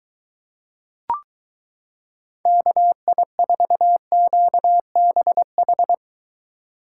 Morse code, heard as a keyed tone, about 23 wpm, 700 Hz. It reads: KI4QBH